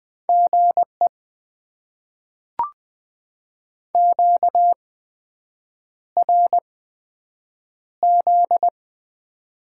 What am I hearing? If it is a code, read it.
Morse code, heard as a keyed tone, about 20 wpm, spaced as for 7 wpm, 700 Hz. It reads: GE QRZ
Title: Top 90 QSO Elements - 8x Character Spacing - 20wpm